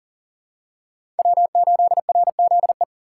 Morse code, heard as a keyed tone, about 40 wpm, 700 Hz. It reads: W8RZE
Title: USA Call Signs - 5 Characters - 2x Character Spacing - 40wpm